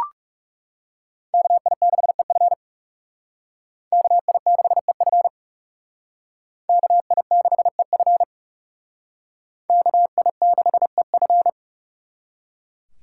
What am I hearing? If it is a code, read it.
Morse code, heard as a keyed tone, about 37 wpm, 700 Hz. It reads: KI6EF KI6EF KI6EF KI6EF
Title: Speed Racing - USA 2x2 Call Signs 30wpm